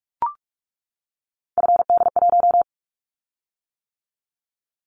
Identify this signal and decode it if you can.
Morse code, heard as a keyed tone, about 46 wpm, 690 Hz. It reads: FD1
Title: Callsign Prefixes for Top 50 DXCC - 45wpm